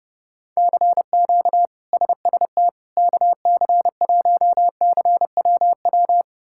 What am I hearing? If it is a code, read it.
Morse code, heard as a keyed tone, about 30 wpm, 700 Hz. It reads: CQ SST KC1CWW
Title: K1USN SST QSO Practice - 30wpm